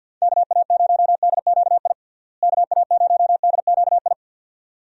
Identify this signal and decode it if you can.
Morse code, heard as a keyed tone, about 52 wpm, 700 Hz. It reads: KA0DQI KA0DQI
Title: USA Call Signs - 6 Characters - Repeated 2x - 50wpm